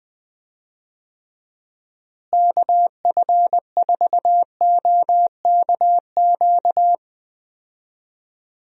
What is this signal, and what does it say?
Morse code, heard as a keyed tone, about 20 wpm, 700 Hz. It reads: KF4OKQ
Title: USA Call Signs - 6 Characters - Repeated 3x - 20wpm